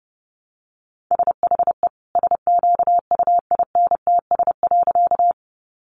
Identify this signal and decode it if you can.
Morse code, heard as a keyed tone, about 30 wpm, 700 Hz. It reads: SHE SQUINTS.
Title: Additional Easy Phrases 30wpm